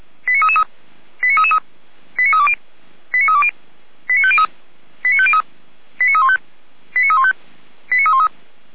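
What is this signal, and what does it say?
Five-tone selective call (ZVEI), heard as quick runs of five tones, tones of about 70 ms. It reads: 89333 89333 89320 89320 89503 89503 89315 89315 89313
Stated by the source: ZVEI five-tone selective call received on SDR, logged by the Feuerwehr Orlamünde fire brigade server.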